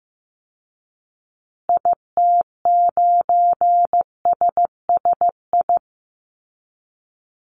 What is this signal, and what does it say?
Morse code, heard as a keyed tone, about 15 wpm, 700 Hz. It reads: IT9SSI
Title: Call Signs Repeated 2x - 15wpm